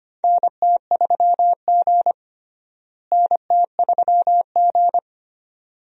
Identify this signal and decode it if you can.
Morse code, heard as a keyed tone, about 25 wpm, 700 Hz. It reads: NT3G NT3G
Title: USA Call Signs - 4 Characters - Repeated 2x - 25wpm